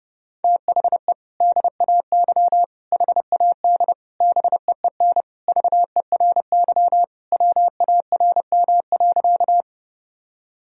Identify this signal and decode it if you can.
Morse code, heard as a keyed tone, about 30 wpm, 700 Hz. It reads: THE DAY HAD BEEN VERY WARM.
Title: Sentences from Top 1000 Words - Review Part 2 - 30wpm